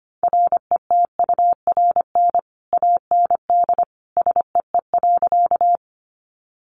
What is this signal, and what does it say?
Morse code, heard as a keyed tone, about 25 wpm, 700 Hz. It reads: RETURN AND SEE.
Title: Sentences from Top 800 Words 25wpm